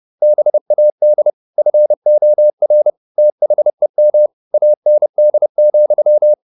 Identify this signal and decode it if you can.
Morse code, heard as a keyed tone, about 30 wpm, 600 Hz. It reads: BAD FOR THEM AND,
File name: the old man and the sea / the old man and the sea 1 30wpm